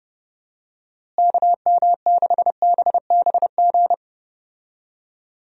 Morse code, heard as a keyed tone, about 30 wpm, 700 Hz. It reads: KM6BBG
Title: USA Call Signs - 6 Characters - Repeated 2x - 30wpm